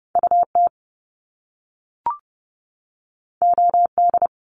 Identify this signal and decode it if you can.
Morse code, heard as a keyed tone, about 30 wpm, 700 Hz. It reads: UT OD
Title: Top Two Letter Combinations in Words 30wpm